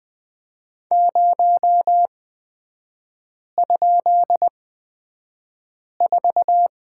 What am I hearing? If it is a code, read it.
Morse code, heard as a keyed tone, about 20 wpm, spaced as for 5 wpm, 700 Hz. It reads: 0?4
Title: 224 Lesson 36e - BT - Three Characters 20wpm